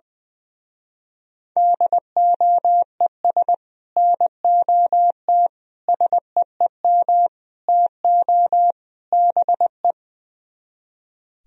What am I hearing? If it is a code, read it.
Morse code, heard as a keyed tone, about 20 wpm, 700 Hz. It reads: DOES NOT SEEM TO BE
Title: Speed-Racing - Sets of 5 Words 20wpm